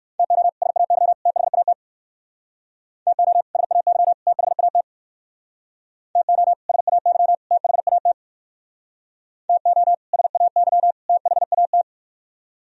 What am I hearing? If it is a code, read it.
Morse code, heard as a keyed tone, about 62 wpm, 700 Hz. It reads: TO SAY THAT TO SAY THAT TO SAY THAT TO SAY THAT